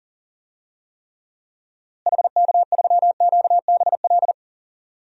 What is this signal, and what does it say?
Morse code, heard as a keyed tone, about 40 wpm, 700 Hz. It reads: HK3QBL
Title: DXCC-100 Call Signs - 6 Characters - Repeated 2x - 40wpm